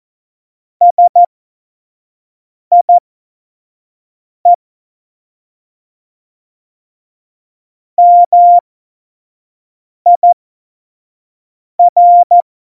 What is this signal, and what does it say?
Morse code, heard as a keyed tone, about 14 wpm, spaced as for 5 wpm, 700 Hz. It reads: SIE MIR